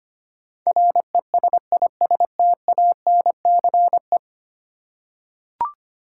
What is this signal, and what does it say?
Morse code, heard as a keyed tone, about 25 wpm, 700 Hz. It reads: RESISTANCE
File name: 10 Letter Words - First 3 Letters Given - 25wpm